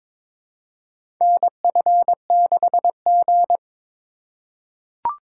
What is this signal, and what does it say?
Morse code, heard as a keyed tone, about 22 wpm, 700 Hz. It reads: NF6G
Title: USA Call Signs - 4 Characters - 3x Character Spacing - 22wpm